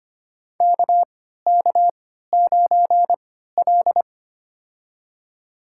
Morse code, heard as a keyed tone, about 25 wpm, spaced as for 14 wpm, 700 Hz. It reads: KK9L
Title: USA Call Signs - 4 Characters - 3x Character Spacing - 25wpm